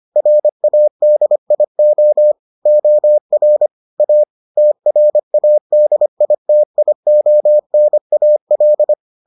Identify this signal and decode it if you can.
Morse code, heard as a keyed tone, about 25 wpm, 600 Hz. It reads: RADIO OR A TRADITIONAL